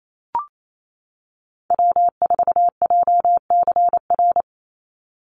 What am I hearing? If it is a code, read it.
Morse code, heard as a keyed tone, about 28 wpm, 700 Hz. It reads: W4JCR
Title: USA Call Signs 28wpm